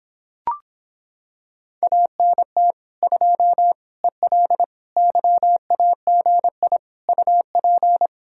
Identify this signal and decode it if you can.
Morse code, heard as a keyed tone, about 26 wpm, 700 Hz. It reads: ANT 2 EL YAGI UP